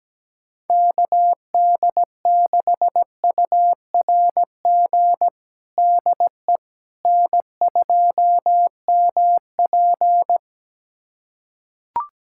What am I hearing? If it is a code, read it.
Morse code, heard as a keyed tone, about 17 wpm, 700 Hz. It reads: KD6URG DE N2MP
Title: USA Callsign DE Callsign 17wpm